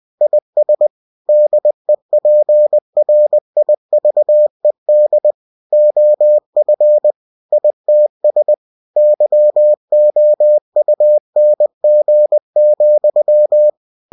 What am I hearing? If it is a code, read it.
Morse code, heard as a keyed tone, about 20 wpm, 600 Hz. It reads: IS DEPRIVED OF ITS YOUNG,
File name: cw-books / winnie-the-pooh / winnie-20wpm-page-0062